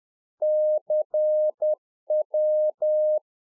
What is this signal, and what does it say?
Morse code, heard as a keyed tone, about 10 wpm, 610 Hz. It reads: CW